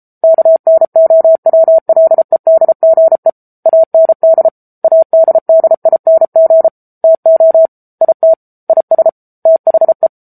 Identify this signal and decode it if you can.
Morse code, heard as a keyed tone, about 34 wpm, 650 Hz. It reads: KNOWLEDGE AND ADDING TO IT IS THE
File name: Book Two / 07-Chapter 7 - 33 WPM